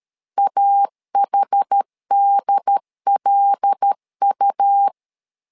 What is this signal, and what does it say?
Morse code, heard as a keyed tone, about 13 wpm, 800 Hz. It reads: AHDLU
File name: LEZIONI / 5 LBVC / 5 L 12wpm 1913